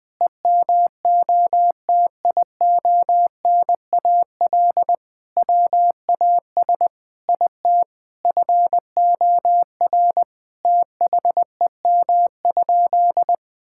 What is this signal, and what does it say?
Morse code, heard as a keyed tone, about 21 wpm, 700 Hz. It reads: EMOTIONAL WAS IT FOR THEM?
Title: Sentences from Top 1000 Words - Review Part 5 - 20wpm